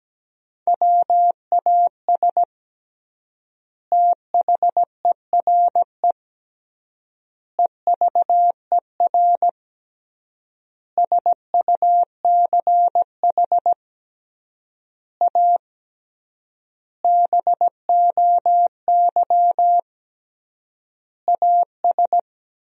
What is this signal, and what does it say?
Morse code, heard as a keyed tone, about 17 wpm, 700 Hz. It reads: WAS THERE EVER SUCH A BOY AS
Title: 3x Word-Spacing - Sentences from Top 400 Words - 17wpm